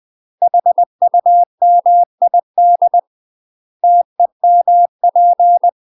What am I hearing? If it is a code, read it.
Morse code, heard as a keyed tone, about 20 wpm, 700 Hz. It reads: HUMID TEMP
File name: UK MASTER QSOs TRAINING / 20WPM / UK MASTER QSOs 0024